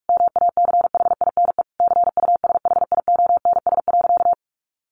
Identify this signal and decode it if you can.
Morse code, heard as a keyed tone, about 44 wpm, 700 Hz. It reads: MACHINE CUSHIONS.
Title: A Sound Of Thunder 45wpm